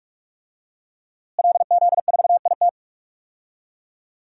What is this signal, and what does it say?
Morse code, heard as a keyed tone, about 46 wpm, 700 Hz. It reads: LZ4IT
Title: DXCC-100 Call Signs - 5 Characters - Repeated 2x - 45wpm